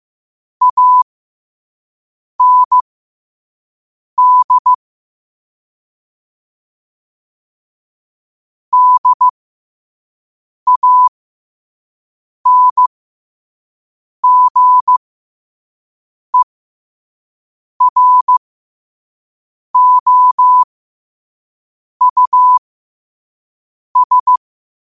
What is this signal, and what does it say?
Morse code, heard as a keyed tone, about 15 wpm, spaced as for 5 wpm, 1000 Hz. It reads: AND DANGEROUS
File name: qotdfiles / 5wpm